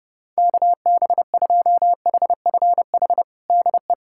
Morse code, heard as a keyed tone, about 30 wpm, 700 Hz. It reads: KB2HFH DE
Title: USA Callsign DE Callsign - 6 Characters - 30wpm